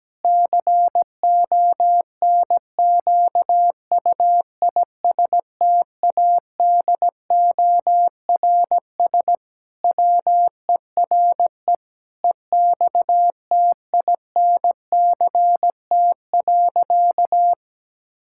Morse code, heard as a keyed tone, about 17 wpm, 700 Hz. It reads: CONQUISTADORS WERE EXTINCT.
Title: Pangram Challenge v2 - 17wpm